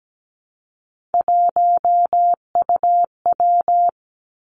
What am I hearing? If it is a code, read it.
Morse code, heard as a keyed tone, about 17 wpm, 700 Hz. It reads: 1UW
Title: Three Challenging Letter-Number Combinations 17wpm